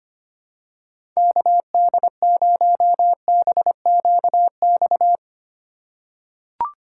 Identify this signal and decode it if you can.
Morse code, heard as a keyed tone, about 25 wpm, 700 Hz. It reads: KD0BQX
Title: USA Call Signs - 6 Characters - 4x Character Spacing - 25wpm